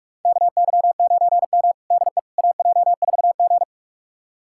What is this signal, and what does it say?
Morse code, heard as a keyed tone, about 42 wpm, 700 Hz. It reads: KY9M DE AJ4G
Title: USA Callsign DE Callsign - 4 Characters - 45wpm